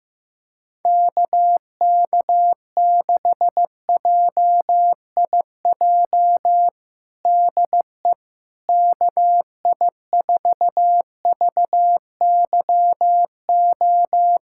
Morse code, heard as a keyed tone, about 16 wpm, 700 Hz. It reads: KK6JIJ DE KI4VYO
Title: USA Callsign DE Callsign - 6 Characters - 15wpm